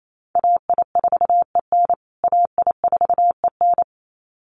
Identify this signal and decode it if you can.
Morse code, heard as a keyed tone, about 28 wpm, 700 Hz. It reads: AI4EN AI4EN
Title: USA Call Signs Repeated 2x 28wpm